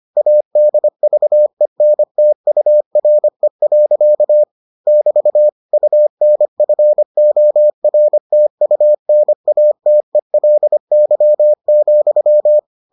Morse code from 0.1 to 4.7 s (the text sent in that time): ADVENTURE.